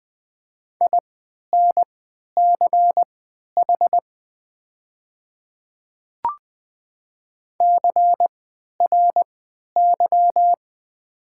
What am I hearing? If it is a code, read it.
Morse code, heard as a keyed tone, about 20 wpm, spaced as for 11 wpm, 690 Hz. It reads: INCH CRY
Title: 3x Character Spacing - Top 500 Words - 20wpm